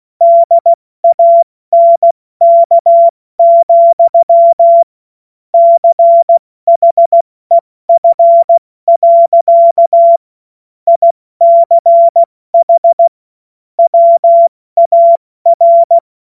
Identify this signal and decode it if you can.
Morse code, heard as a keyed tone, about 16 wpm, 680 Hz. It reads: DANK, CHEF. ICH WAR
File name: Morsekurs / Witze / Witz 04. April 70 F80